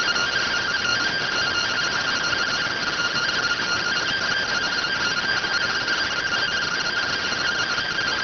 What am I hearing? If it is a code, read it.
Two-tone teleprinter signal, about 170 Hz shift.